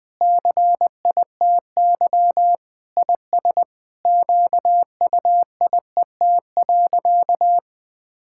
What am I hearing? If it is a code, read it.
Morse code, heard as a keyed tone, about 20 wpm, 700 Hz. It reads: CITY IS QUIET.
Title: Sentences with 7-Word Phrases - 20wpm